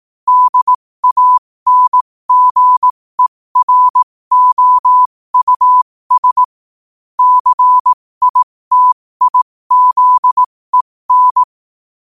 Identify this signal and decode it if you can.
Morse code, heard as a keyed tone, about 18 wpm, spaced as for 15 wpm, 1000 Hz. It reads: DANGEROUS CITIZEN
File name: qotdfiles / 15wpm